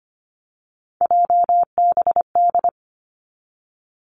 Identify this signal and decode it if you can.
Morse code, heard as a keyed tone, about 25 wpm, 700 Hz. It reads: JBD
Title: Three Challenging Letter-Number Combinations 25wpm